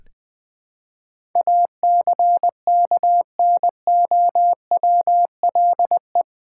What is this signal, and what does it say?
Morse code, heard as a keyed tone, about 20 wpm, 700 Hz. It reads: ACKNOWLE
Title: Build A Word - 4 Letters At A Time - 11-Letter Words - 20wpm